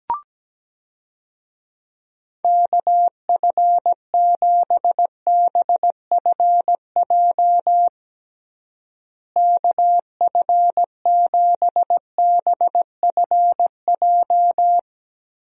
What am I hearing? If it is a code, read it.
Morse code, heard as a keyed tone, about 17 wpm, 700 Hz. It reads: KF7BFJ KF7BFJ
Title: USA Call Signs - 6 Characters - Repeated 2x - 17wpm